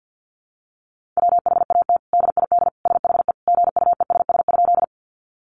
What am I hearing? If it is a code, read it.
Morse code, heard as a keyed tone, about 50 wpm, 700 Hz. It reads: WHAT DID SHE GUESS?
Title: Sentences from Top 1500 Words 50wpm